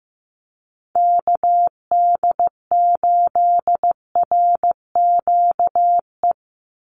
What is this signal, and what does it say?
Morse code, heard as a keyed tone, about 15 wpm, 700 Hz. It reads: KD8RQE
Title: Call Signs Repeated 2x - 15wpm